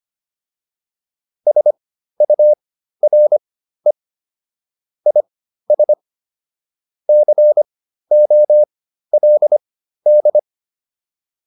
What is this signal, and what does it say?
Morse code, heard as a keyed tone, about 25 wpm, spaced as for 13 wpm, 600 Hz. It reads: SURE IS COLD